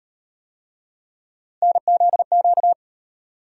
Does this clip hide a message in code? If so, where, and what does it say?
Yes: Morse code from 1.6 to 2.7 s, heard as a keyed tone, about 38 wpm, 700 Hz. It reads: NZQ